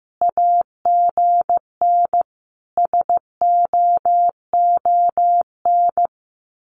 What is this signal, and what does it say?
Morse code, heard as a keyed tone, about 15 wpm, 700 Hz. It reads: AGN SOON
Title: Common QSO Phrases 15wpm